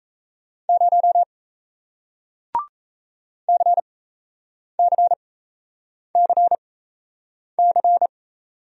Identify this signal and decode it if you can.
Morse code, heard as a keyed tone, about 38 wpm, spaced as for 8 wpm, 700 Hz. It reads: 0 CCCC